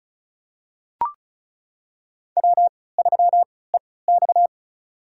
Morse code, heard as a keyed tone, about 34 wpm, spaced as for 20 wpm, 700 Hz. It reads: W3EX
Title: USA Call Signs - 3x Character Spacing - 35wpm